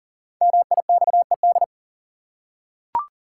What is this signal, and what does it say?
Morse code, heard as a keyed tone, about 40 wpm, 700 Hz. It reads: MIXED
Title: Five Letter Words - 40wpm